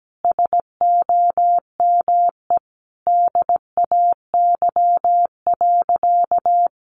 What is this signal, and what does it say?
Morse code, heard as a keyed tone, about 17 wpm, 700 Hz. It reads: SOME DAY.
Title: Sentences from Top 900 Words 17wpm